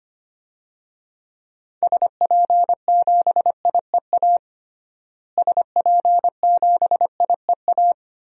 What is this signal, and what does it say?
Morse code, heard as a keyed tone, about 25 wpm, 700 Hz. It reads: SP7IEA SP7IEA
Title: DXCC-100 Call Signs - 6 Characters - Repeated 3x - 25wpm